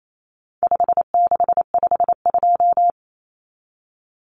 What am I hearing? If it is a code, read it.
Morse code, heard as a keyed tone, about 28 wpm, 700 Hz. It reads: 5652
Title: Common Cut Numbers - 4-Digit 28wpm